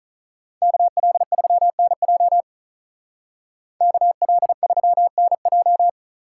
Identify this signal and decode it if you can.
Morse code, heard as a keyed tone, about 46 wpm, 700 Hz. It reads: KL3NJ KL3NJ